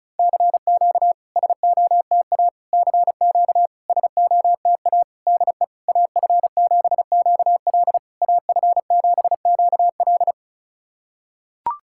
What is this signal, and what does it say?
Morse code, heard as a keyed tone, about 36 wpm, 700 Hz. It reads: CQ SOTA CQ SOTA DE AF7QL AF7QL
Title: SOTA QSO Practice 35wpm